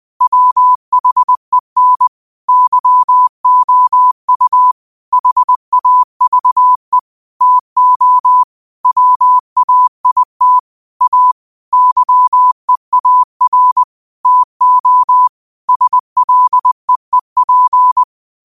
Morse code, heard as a keyed tone, about 20 wpm, 1000 Hz. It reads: WHEN YOU HAVE TO WAIT A YEAR TO SLEEP
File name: qotdfiles / 20wpm